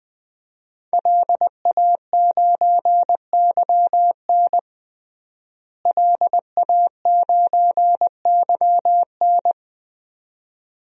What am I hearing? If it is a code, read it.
Morse code, heard as a keyed tone, about 20 wpm, 700 Hz. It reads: LA9YN LA9YN
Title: DXCC-100 Call Signs - 5 Characters - Repeated 2x - 20wpm